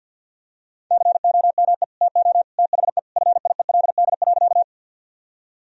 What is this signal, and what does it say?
Morse code, heard as a keyed tone, about 50 wpm, 700 Hz. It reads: COME TO THE FIELD.
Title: Sentences from Top 1000 Words - Review Part 2 - 50wpm